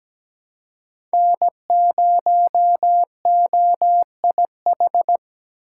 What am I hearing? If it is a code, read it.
Morse code, heard as a keyed tone, about 17 wpm, 700 Hz. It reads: N0OIH
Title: USA Call Signs - 5 Characters - 3x Character Spacing - 17wpm